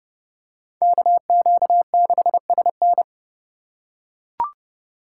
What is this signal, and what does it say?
Morse code, heard as a keyed tone, about 30 wpm, 700 Hz. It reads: KQ6SN